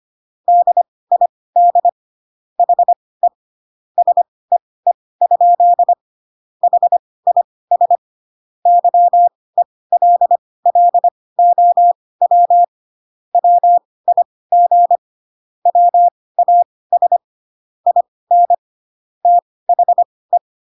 Morse code, heard as a keyed tone, about 25 wpm, spaced as for 18 wpm, 700 Hz. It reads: DID HE SEE? HIS YELLOW WIG WAS IN THE
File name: Pinocchio ch3 12 25 ik1hge